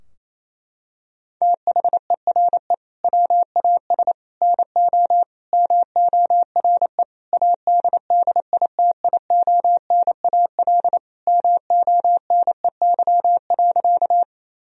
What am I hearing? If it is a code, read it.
Morse code, heard as a keyed tone, about 28 wpm, 700 Hz. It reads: THERE WAS NO MORE ADDITIONAL MONEY.
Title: Sentences from Top 1000 Words - Review Part 4 - 28wpm